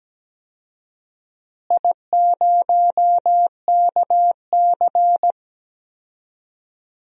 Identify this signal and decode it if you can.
Morse code, heard as a keyed tone, about 17 wpm, 700 Hz. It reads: I0KC
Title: DXCC-100 Call Signs - 4 Characters - Repeated 2x - 17wpm